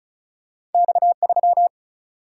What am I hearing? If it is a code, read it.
Morse code, heard as a keyed tone, about 36 wpm, 700 Hz. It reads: X3
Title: Single-Letter Single-Number 35wpm